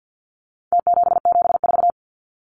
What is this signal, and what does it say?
Morse code, heard as a keyed tone, about 48 wpm, 700 Hz. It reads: T774
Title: Numbers - Contest Style - 4-Digit 50wpm